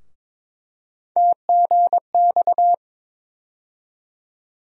Morse code, heard as a keyed tone, about 22 wpm, 700 Hz. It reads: TGX
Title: USA Call Sign Postfixes - 3 Letters 22wpm